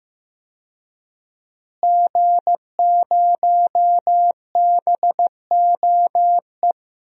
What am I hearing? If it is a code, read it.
Morse code, heard as a keyed tone, about 15 wpm, 700 Hz. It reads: G0BOE
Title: DXCC-100 Call Signs - 5 Characters - Repeated 2x - 15wpm